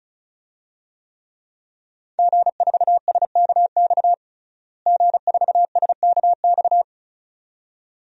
Morse code, heard as a keyed tone, about 34 wpm, 700 Hz. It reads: G4SKX G4SKX
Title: DXCC-100 Call Signs - 5 Characters - Repeated 2x - 35wpm